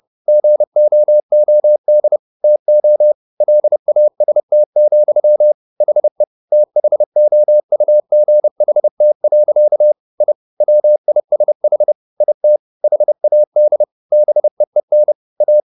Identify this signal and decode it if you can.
Morse code, heard as a keyed tone, about 30 wpm, 600 Hz. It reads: GOOD TO LAST, HE THOUGHT. I WISH IT HAD BEEN A